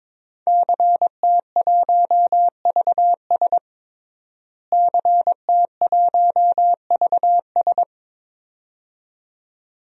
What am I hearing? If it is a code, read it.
Morse code, heard as a keyed tone, about 22 wpm, 700 Hz. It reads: CT1VS CT1VS
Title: DXCC-100 Call Signs - 5 Characters - Repeated 2x - 22wpm